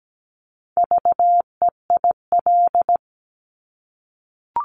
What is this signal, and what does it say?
Morse code, heard as a keyed tone, about 17 wpm, 700 Hz. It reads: VEIL
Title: Multi-letter Phonemes Repeated 3x 17wpm